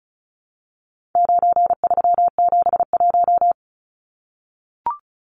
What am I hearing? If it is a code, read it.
Morse code, heard as a keyed tone, about 34 wpm, 700 Hz. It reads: 9371
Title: Numbers - 4-Digit 35wpm